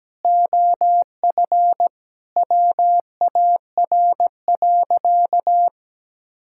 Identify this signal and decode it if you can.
Morse code, heard as a keyed tone, about 17 wpm, 700 Hz. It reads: OF WAR.
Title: Sentences from Top 1000 Words - Review Part 6 - 17wpm